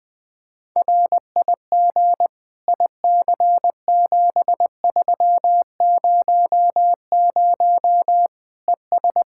Morse code, heard as a keyed tone, about 20 wpm, 700 Hz. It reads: RIG IC7300 ES
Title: Practice QSO - Exchange 2 - Info - 20wpm